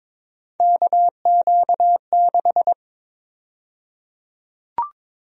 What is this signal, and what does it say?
Morse code, heard as a keyed tone, about 22 wpm, 700 Hz. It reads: KQ6
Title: USA Call Sign Prefixes - Repeated 2x - 22wpm